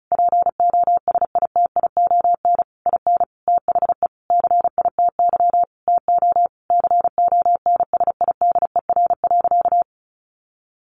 Morse code, heard as a keyed tone, about 34 wpm, 700 Hz. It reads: POSITION IN THE CITY TO CONSIDER.
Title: Sentences from Top 900 Words 35wpm